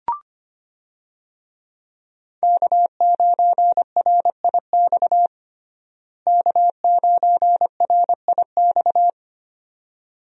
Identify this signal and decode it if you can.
Morse code, heard as a keyed tone, about 25 wpm, 700 Hz. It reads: K9RIX K9RIX
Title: USA Call Signs - 5 Characters - Repeated 2x - 25wpm